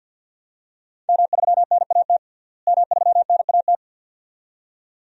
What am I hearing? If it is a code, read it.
Morse code, heard as a keyed tone, about 47 wpm, 700 Hz. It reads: M3NAT M3NAT
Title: DXCC-100 Call Signs - 5 Characters - Repeated 2x - 50wpm